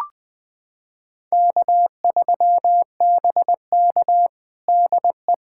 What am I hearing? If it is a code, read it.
Morse code, heard as a keyed tone, about 20 wpm, 700 Hz. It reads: K3BK DE
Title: USA Callsign DE Callsign - 4 Characters - 20wpm